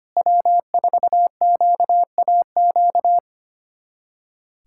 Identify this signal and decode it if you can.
Morse code, heard as a keyed tone, about 25 wpm, 700 Hz. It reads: W4QAQ
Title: USA Callsign DE Callsign - 5 Characters - 25wpm